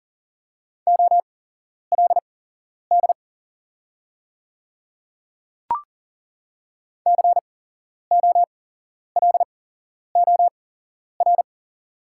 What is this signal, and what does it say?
Morse code, heard as a keyed tone, about 40 wpm, spaced as for 11 wpm, 700 Hz. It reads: OLD COLOR
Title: 8x Character Spacing - Top 300 Words - 40wpm